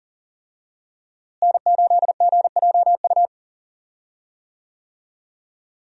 Morse code, heard as a keyed tone, about 40 wpm, 700 Hz. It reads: N8GJU
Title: USA Call Signs - 5 Characters - Repeated 2x - 40wpm